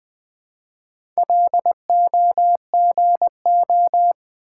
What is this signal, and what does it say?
Morse code, heard as a keyed tone, about 20 wpm, 690 Hz. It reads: LOGO
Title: Build A Word - 1 Letter At A Time - 4-Letter Words - 20wpm